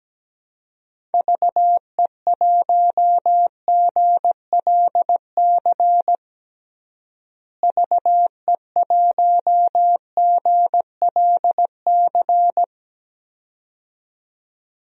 Morse code, heard as a keyed tone, about 17 wpm, 700 Hz. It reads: VE1GLC VE1GLC
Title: DXCC-100 Call Signs - 6 Characters - Repeated 2x - 17wpm